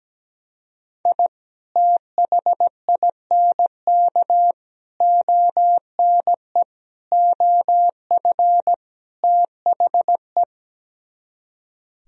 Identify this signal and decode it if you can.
Morse code, heard as a keyed tone, about 17 wpm, 700 Hz. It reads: I THINK ONE OF THE
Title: Speed-Racing - Sets of 5 Words 17wpm